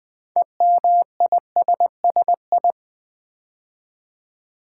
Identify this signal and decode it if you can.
Morse code, heard as a keyed tone, about 20 wpm, 700 Hz. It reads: EMISSI
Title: Build A Word - 3 Letters At A Time - 8-Letter Words - 20wpm